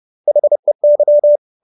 Morse code, heard as a keyed tone, about 30 wpm, 590 Hz. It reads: HEY